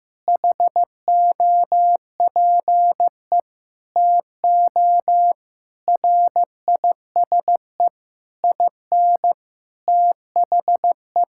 Morse code, heard as a keyed tone, about 15 wpm, 700 Hz. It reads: HOPE TO RISE IN THE
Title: Sentences from Top 1000 Words - Review Part 6 - 15wpm